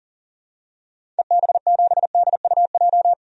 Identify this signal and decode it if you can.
Morse code, heard as a keyed tone, about 37 wpm, 700 Hz. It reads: EB7DUJ
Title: DXCC-100 Call Signs - 6 Characters - 40wpm